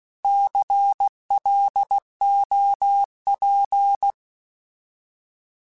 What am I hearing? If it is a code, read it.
Morse code, heard as a keyed tone, about 16 wpm, 800 Hz. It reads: CLOP
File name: Alphabetical / Alphabetical mp3s / Cs @15